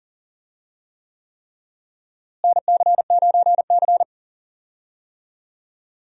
Morse code, heard as a keyed tone, about 40 wpm, 700 Hz. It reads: NC9C
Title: USA Call Signs - 4 Characters - Repeated 3x - 40wpm